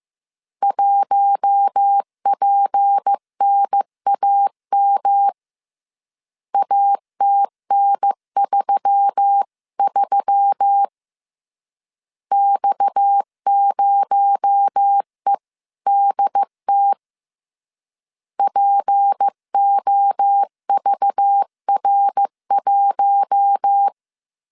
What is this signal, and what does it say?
Morse code, heard as a keyed tone, about 15 wpm, 800 Hz. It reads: 1PNAM ATN33 X0EDT POVR1
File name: CW / DET / 1 6 1500C S C 14wpm 2951